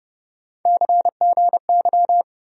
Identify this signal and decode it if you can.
Morse code, heard as a keyed tone, about 32 wpm, 700 Hz. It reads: CGY
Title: USA Call Sign Postfixes - 3 Letters 30wpm